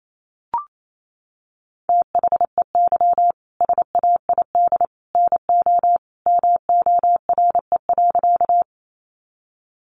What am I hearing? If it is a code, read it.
Morse code, heard as a keyed tone, about 28 wpm, 700 Hz. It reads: THEY SAID NO MORE.
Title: Sentences from Top 100 Words 28wpm